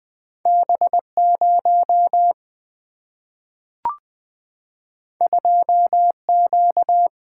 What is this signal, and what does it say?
Morse code, heard as a keyed tone, about 20 wpm, 700 Hz. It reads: B 0 2 Q
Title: Two Letter-Number Combinations - 20wpm